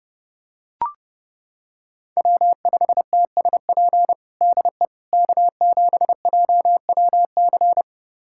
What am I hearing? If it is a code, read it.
Morse code, heard as a keyed tone, about 30 wpm, 700 Hz. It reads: W5TSP DE K7JWC
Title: USA Callsign DE Callsign - 5 Characters - 30wpm